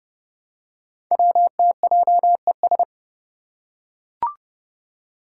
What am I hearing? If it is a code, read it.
Morse code, heard as a keyed tone, about 30 wpm, 700 Hz. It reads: WTJES
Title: Code Groups - 5 Letters - 30wpm